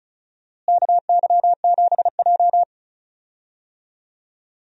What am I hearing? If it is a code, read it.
Morse code, heard as a keyed tone, about 36 wpm, 700 Hz. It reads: KY7J
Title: USA Call Signs - 4 Characters - Repeated 3x - 35wpm